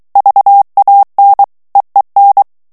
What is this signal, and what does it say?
Morse code, heard as a keyed tone, about 23 wpm, 800 Hz. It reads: VAN EEN
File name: HOLLANDAIS texte clair 1 vitesse 22 mots minute